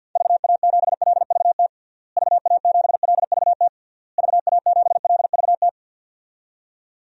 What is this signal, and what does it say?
Morse code, heard as a keyed tone, about 48 wpm, 700 Hz. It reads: VA7LVT VA7LVT VA7LVT
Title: DXCC-100 Call Signs - 6 Characters - Repeated 3x - 50wpm